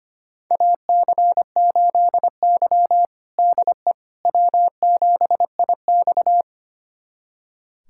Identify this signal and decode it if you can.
Morse code, heard as a keyed tone, about 25 wpm, 700 Hz. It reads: AC8Y DE W7IX
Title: USA Callsign DE Callsign - 4 Characters - 25wpm